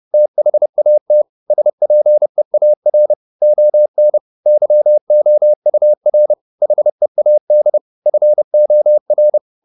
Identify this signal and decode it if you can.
Morse code, heard as a keyed tone, about 30 wpm, 600 Hz. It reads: THAT SPEAR ON YOUR HEAD FOR